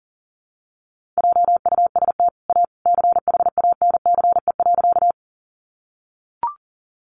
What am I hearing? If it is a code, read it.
Morse code, heard as a keyed tone, about 40 wpm, 700 Hz. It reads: JUST A CHANCE.
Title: Sentences from Top 800 Words 40wpm